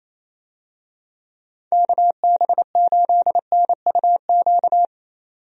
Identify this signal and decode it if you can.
Morse code, heard as a keyed tone, about 28 wpm, 700 Hz. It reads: KB8NUQ
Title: USA Call Signs - 6 Characters - 28wpm